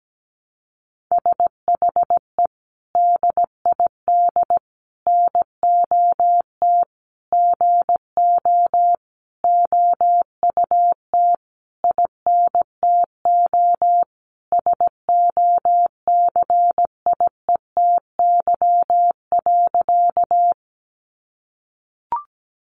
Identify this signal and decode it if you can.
Morse code, heard as a keyed tone, about 17 wpm, 700 Hz. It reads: SHE DID NOT GO OUT INTO SOCIETY.
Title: Sentences from Top 700 Words 17wpm